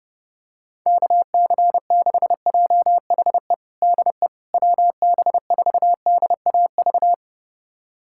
Morse code, heard as a keyed tone, about 30 wpm, 700 Hz. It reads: KC6JHE DE WB4DAV
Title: USA Callsign DE Callsign 30wpm